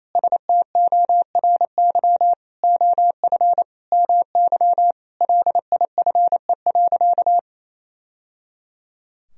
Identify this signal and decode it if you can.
Morse code, heard as a keyed tone, about 27 wpm, 700 Hz. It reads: STORY OF MY LIFE.
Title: Sentences from Top 1000 Words - Review Part 1 - 28wpm